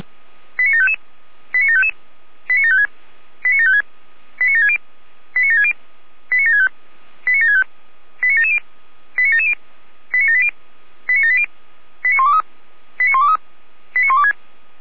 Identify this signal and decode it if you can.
Five-tone selective call (ZVEI), heard as quick runs of five tones, tones of about 70 ms. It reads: 89755 89755 89756 89756 89760 89760 89765 89765 89770 89770 89780 89780 89123 89123 89126